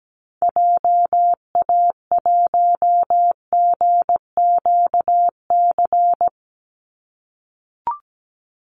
Morse code, heard as a keyed tone, about 17 wpm, 700 Hz. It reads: JA1GQC
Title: Call Signs - 17wpm